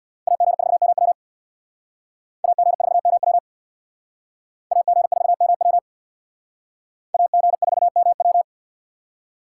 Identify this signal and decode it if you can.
Morse code, heard as a keyed tone, about 64 wpm, 700 Hz. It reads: AG4MW AG4MW AG4MW AG4MW